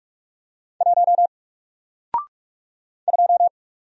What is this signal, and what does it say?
Morse code, heard as a keyed tone, about 44 wpm, 700 Hz. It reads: 1 2
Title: Single Character (Letter:Number:Punctuation:Prosign) 45wpm